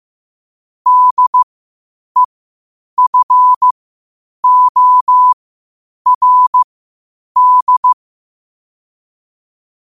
Morse code, heard as a keyed tone, about 15 wpm, spaced as for 8 wpm, 1000 Hz. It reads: DEFORD
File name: qotdfiles / 8wpm